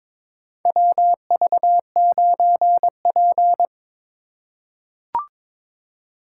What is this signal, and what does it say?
Morse code, heard as a keyed tone, about 22 wpm, 700 Hz. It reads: WV9P
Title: USA Call Signs - 4 Characters - 4x Character Spacing - 22wpm